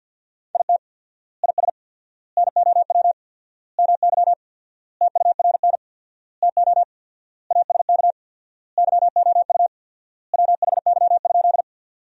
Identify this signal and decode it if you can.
Morse code, heard as a keyed tone, about 52 wpm, 700 Hz. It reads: IT IS NOW MY TURN TO ASK YOU WHY?
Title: 4x Word-Spacing - Sentences from Top 200 Words - 50wpm